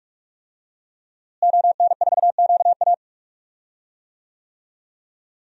Morse code, heard as a keyed tone, about 42 wpm, 700 Hz. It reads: ON4QA
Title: DXCC-100 Call Signs - 5 Characters - Repeated 2x - 45wpm